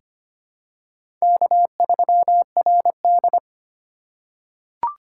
Morse code, heard as a keyed tone, about 25 wpm, 700 Hz. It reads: K3RD
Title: USA Call Signs - 4 Characters - 4x Character Spacing - 25wpm